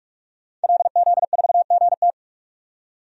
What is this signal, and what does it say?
Morse code, heard as a keyed tone, about 41 wpm, 700 Hz. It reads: LZ4GT
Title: DXCC-100 Call Signs - 5 Characters - 45wpm